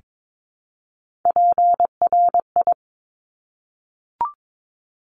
Morse code, heard as a keyed tone, about 22 wpm, 700 Hz. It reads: PRI
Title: Top Three Letter Combinations in Words 22wpm